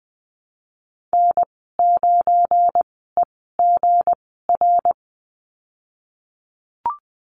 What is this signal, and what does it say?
Morse code, heard as a keyed tone, about 20 wpm, spaced as for 15 wpm, 700 Hz. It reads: N9EGR
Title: 186 Lesson 30g - Letter b - Call Signs 20wpm